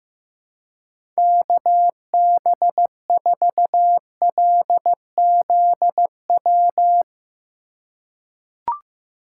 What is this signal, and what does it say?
Morse code, heard as a keyed tone, about 15 wpm, 700 Hz. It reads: KB4LZW
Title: USA Call Signs - 2x Character Spacing - 15wpm